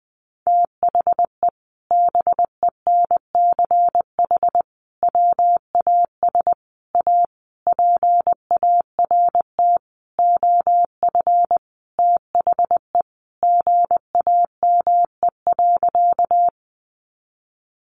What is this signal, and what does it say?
Morse code, heard as a keyed tone, about 20 wpm, 700 Hz. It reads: THE BENCH WAS A PART OF THE GAME.
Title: Sentences from Top 1400 Words 20wpm